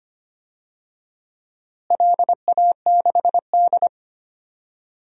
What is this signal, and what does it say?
Morse code, heard as a keyed tone, about 25 wpm, 700 Hz. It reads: LA6D
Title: DXCC-100 Call Signs - 4 Characters - Repeated 2x - 25wpm